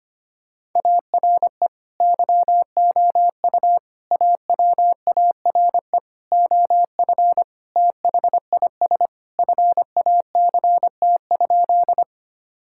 Morse code, heard as a keyed tone, about 25 wpm, 690 Hz. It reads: ARE YOU AWARE OF THIS FACT?
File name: practice / 25wpm / Sentences with 2-Word Phrases - 25wpm